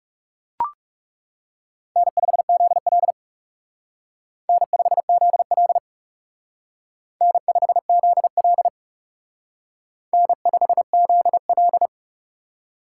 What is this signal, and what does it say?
Morse code, heard as a keyed tone, about 39 wpm, 700 Hz. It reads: N5ZL N5ZL N5ZL N5ZL